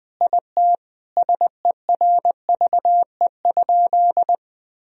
Morse code, heard as a keyed tone, about 21 wpm, 700 Hz. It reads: IT SERVE?
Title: Sentences from Top 1000 Words - Review Part 2 - 20wpm